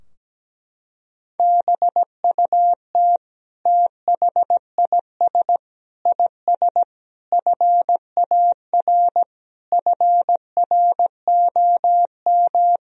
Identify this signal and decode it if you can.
Morse code, heard as a keyed tone, about 17 wpm, 700 Hz. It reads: BUT THIS IS FAR FROM
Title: Sentences from Top 1000 Words - Review Part 3 - 17wpm